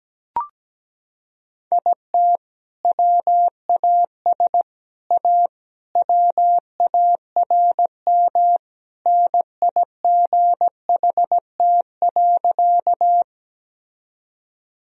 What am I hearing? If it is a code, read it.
Morse code, heard as a keyed tone, about 17 wpm, 700 Hz. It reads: IT WAS A WARM NIGHT.
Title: Sentences from Top 1000 Words - Review Part 2 - 17wpm